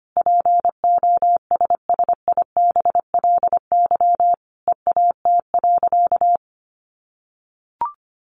Morse code, heard as a keyed tone, about 25 wpm, 700 Hz. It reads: POSSIBLY EAT.
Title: Sentences from Top 1300 Words 25wpm